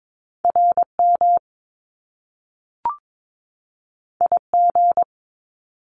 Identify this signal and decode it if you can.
Morse code, heard as a keyed tone, about 22 wpm, 700 Hz. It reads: RM IG
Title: Top Two Letter Combinations in Words 22wpm